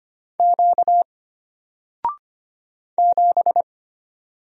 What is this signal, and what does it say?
Morse code, heard as a keyed tone, about 25 wpm, 700 Hz. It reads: Q 7
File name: Single-Letter-Number-25wpm